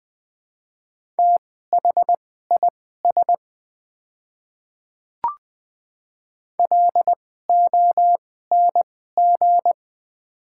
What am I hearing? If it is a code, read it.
Morse code, heard as a keyed tone, about 20 wpm, spaced as for 14 wpm, 700 Hz. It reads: THIS LONG
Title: 2x Character Spacing - Top 100 Words - 20wpm